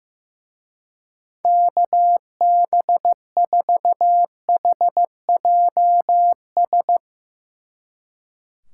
Morse code, heard as a keyed tone, about 16 wpm, 700 Hz. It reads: KB4HJS
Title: Speed Racing - USA 2x3 Call Signs 15wpm